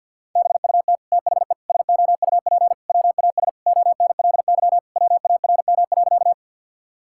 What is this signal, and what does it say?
Morse code, heard as a keyed tone, about 50 wpm, 700 Hz. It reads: BUT THE SOUP WAS ONLY WARM.
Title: Sentences from Top 1000 Words - Review Part 6 - 50wpm